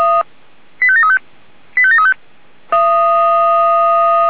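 Five-tone selective call (ZVEI), heard as quick runs of five tones, tones of about 70 ms. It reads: 85735 85735
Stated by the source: Five-tone ZVEI selective call received on an SDR.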